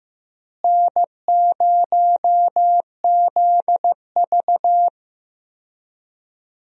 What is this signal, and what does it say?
Morse code, heard as a keyed tone, about 15 wpm, 700 Hz. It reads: N0ZV
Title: USA Call Signs - 4 Characters - Repeated 2x - 15wpm